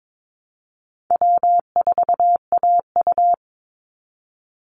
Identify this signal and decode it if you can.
Morse code, heard as a keyed tone, about 22 wpm, 700 Hz. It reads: W4AU
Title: Call Signs - 22wpm